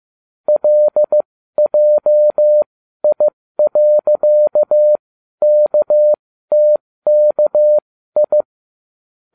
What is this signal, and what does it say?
Morse code, heard as a keyed tone, about 15 wpm, spaced as for 12 wpm, 600 Hz. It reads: LJI.KTKI